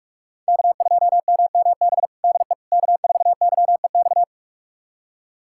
Morse code, heard as a keyed tone, about 46 wpm, 700 Hz. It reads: K2MMB DE K4YEX
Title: USA Callsign DE Callsign - 5 Characters - 45wpm